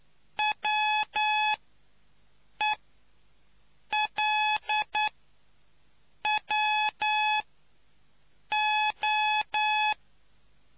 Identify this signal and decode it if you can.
Morse code, heard as a keyed tone, about 10 wpm, spaced as for 5 wpm, 850 Hz. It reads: WELWO